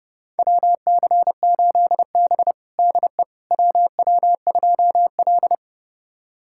Morse code, heard as a keyed tone, about 30 wpm, 700 Hz. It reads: WC8B DE WW2L
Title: USA Callsign DE Callsign - 4 Characters - 30wpm